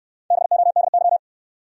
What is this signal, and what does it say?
Morse code, heard as a keyed tone, about 68 wpm, 700 Hz. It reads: BODY